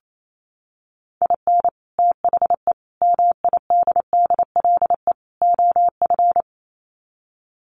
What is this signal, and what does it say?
Morse code, heard as a keyed tone, about 28 wpm, 700 Hz. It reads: IN THE MIDDLE OF
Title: Sets of 4 Words 28wpm